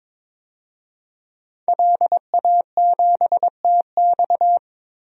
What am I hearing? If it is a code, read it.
Morse code, heard as a keyed tone, about 22 wpm, 700 Hz. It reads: LA7TX
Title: DXCC-100 Call Signs - 5 Characters - 22wpm